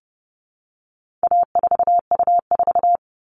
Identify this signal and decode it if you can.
Morse code, heard as a keyed tone, about 30 wpm, 700 Hz. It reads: A4U4
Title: All Cut Numbers - 4-Digit 30wpm